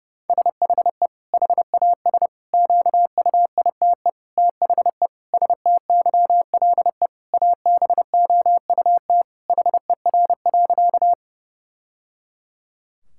Morse code, heard as a keyed tone, about 30 wpm, 700 Hz. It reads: SHE HAS QUITE THE STYLE ABOUT HER.
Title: Sentences from Top 1000 Words - Review Part 5 - 30wpm